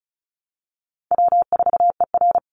Morse code, heard as a keyed tone, about 34 wpm, 700 Hz. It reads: W4ER
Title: Easy Call Signs Repeated 3x - 35wpm